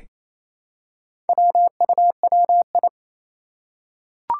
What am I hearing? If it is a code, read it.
Morse code, heard as a keyed tone, about 27 wpm, 700 Hz. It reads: WUWI
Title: Four Challenging Letter-Number Combinations 28wpm